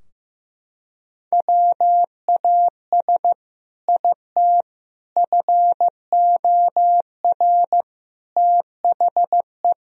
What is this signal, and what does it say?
Morse code, heard as a keyed tone, about 16 wpm, 700 Hz. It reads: WAS IT FOR THE
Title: Sentences from Top 1000 Words - Review Part 3 - 15wpm